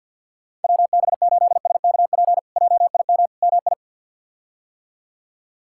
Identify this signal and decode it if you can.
Morse code, heard as a keyed tone, about 51 wpm, 700 Hz. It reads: WB8SKP JIM MI
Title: K1USN SST QSO Practice - 50wpm